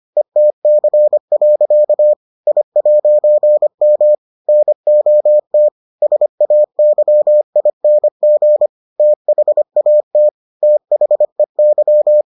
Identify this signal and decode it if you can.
Morse code, heard as a keyed tone, about 25 wpm, 600 Hz. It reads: ETC. I'M NOT SAYING THAT THEY